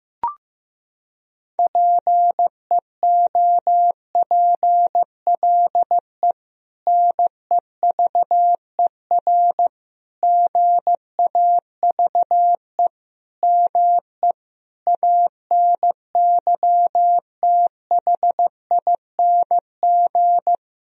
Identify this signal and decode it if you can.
Morse code, heard as a keyed tone, about 16 wpm, 700 Hz. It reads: PEOPLE NEVER GAVE ME ANYTHING
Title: Sentences from Top 1000 Words - Review Part 1 - 15wpm